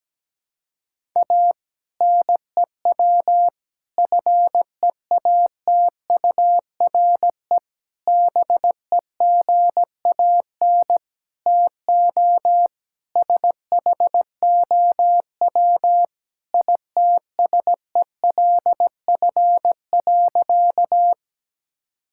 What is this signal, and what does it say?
Morse code, heard as a keyed tone, about 17 wpm, 700 Hz. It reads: A NEW FEATURE BEGAN TO SHOW ITSELF.
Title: Sentences from Top 1000 Words - Review Part 5 - 17wpm